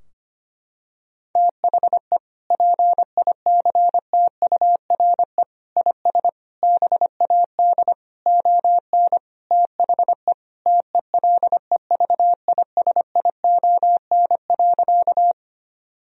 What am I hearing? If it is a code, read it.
Morse code, heard as a keyed tone, about 25 wpm, 700 Hz. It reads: THE PICTURE IS BAD ON THE TELEVISION.
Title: Sentences from Top 1000 Words - Review Part 4 - 25wpm